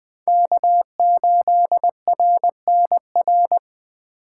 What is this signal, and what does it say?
Morse code, heard as a keyed tone, about 24 wpm, 700 Hz. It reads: K8RNR
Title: Speed Racing - USA 1x3 Call Signs 20wpm